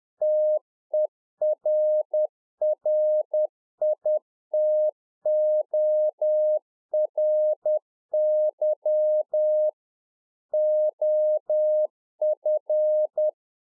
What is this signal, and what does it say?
Morse code, heard as a keyed tone, about 10 wpm, 610 Hz. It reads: TERRITORY OF